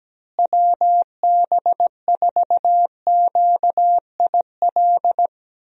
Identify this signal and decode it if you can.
Morse code, heard as a keyed tone, about 17 wpm, 700 Hz. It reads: WB4QIL
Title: USA Call Signs - 6 Characters - Repeated 3x - 17wpm